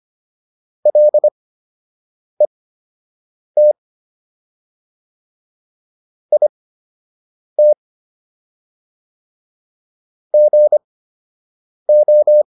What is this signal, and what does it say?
Morse code, heard as a keyed tone, about 25 wpm, spaced as for 7 wpm, 600 Hz. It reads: LET IT GO